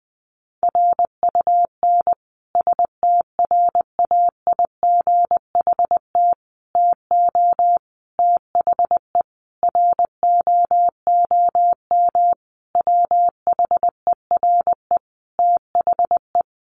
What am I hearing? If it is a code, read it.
Morse code, heard as a keyed tone, about 19 wpm, 700 Hz. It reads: RUN STRAIGHT TO THE ROOM WHERE THE
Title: Sentences from Top 1000 Words 20wpm